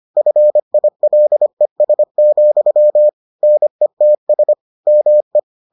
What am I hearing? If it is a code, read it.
Morse code, heard as a keyed tone, about 25 wpm, 600 Hz. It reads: FILES, NETS ME